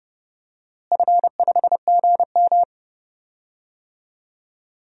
Morse code, heard as a keyed tone, about 30 wpm, 700 Hz. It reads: F5GM
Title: DXCC-100 Call Signs - 4 Characters - Repeated 2x - 30wpm